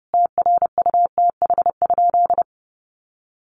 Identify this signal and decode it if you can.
Morse code, heard as a keyed tone, about 30 wpm, 700 Hz. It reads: TRUTH?
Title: Sentences from Top 1300 Words 30wpm